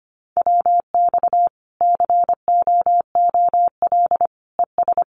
Morse code, heard as a keyed tone, about 25 wpm, 700 Hz. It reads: WX COOL ES
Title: Common QSO Phrases 25wpm